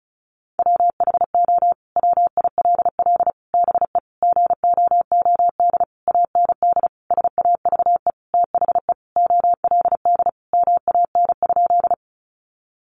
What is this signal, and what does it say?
Morse code, heard as a keyed tone, about 34 wpm, 700 Hz. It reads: WHO WILL BE GOOD AND SAVE THE OLD MAN?
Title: Sentences from Top 1100 Words 35wpm